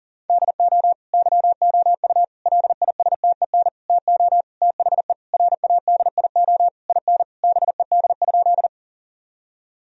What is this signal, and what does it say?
Morse code, heard as a keyed tone, about 40 wpm, 700 Hz. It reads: DO YOU LISTEN TO THE RADIO IN BED?
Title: Sentences from Top 1000 Words - Review Part 3 - 40wpm